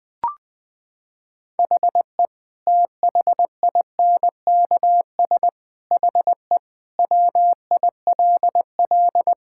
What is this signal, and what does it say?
Morse code, heard as a keyed tone, about 20 wpm, 700 Hz. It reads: HE THINKS HE WILL
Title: Sentences from Top 1000 Words - Review Part 6 - 20wpm